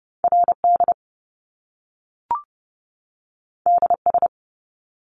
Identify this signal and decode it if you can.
Morse code, heard as a keyed tone, about 30 wpm, 700 Hz. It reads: RD DS
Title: Top Two Letter Combinations in Words 30wpm